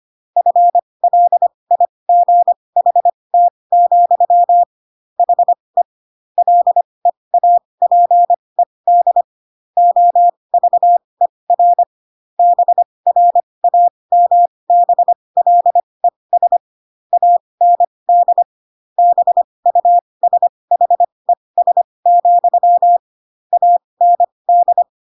Morse code, heard as a keyed tone, about 25 wpm, spaced as for 20 wpm, 700 Hz. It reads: FLIGHT, HE LEAPED OVER BRAMBLES AND BUSHES, AND
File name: Pinocchio / Pinocchio ch4 15 25 ik1hge